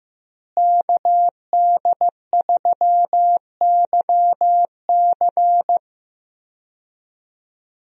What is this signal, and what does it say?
Morse code, heard as a keyed tone, about 15 wpm, 700 Hz. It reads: KD3YC
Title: USA Call Signs - 5 Characters - Repeated 3x - 15wpm